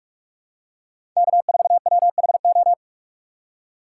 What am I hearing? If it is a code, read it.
Morse code, heard as a keyed tone, about 46 wpm, 700 Hz. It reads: K4WHO
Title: USA Call Signs - 5 Characters - 3x Character Spacing - 45wpm